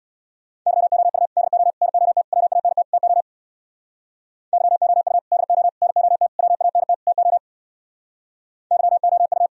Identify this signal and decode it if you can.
Morse code, heard as a keyed tone, about 59 wpm, 700 Hz. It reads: YOU DO NOT WANT TO YOU DO NOT WANT TO YOU
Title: Speed-Racing - Sets of 5 Words 50wpm